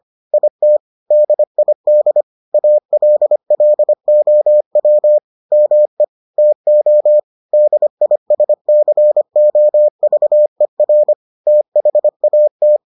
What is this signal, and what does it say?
Morse code, heard as a keyed tone, about 25 wpm, 600 Hz. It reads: IT DID ALLOW ME TO DISCOVER THAT